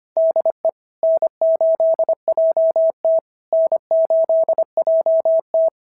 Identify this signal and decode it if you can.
Morse code, heard as a keyed tone, about 25 wpm, 650 Hz. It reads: DE N8JT N8JT
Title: SOTA QSO Practice 25wpm